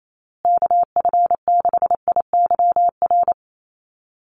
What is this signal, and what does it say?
Morse code, heard as a keyed tone, about 28 wpm, 700 Hz. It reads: KF6IYR
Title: USA Call Signs Repeated 3x 28wpm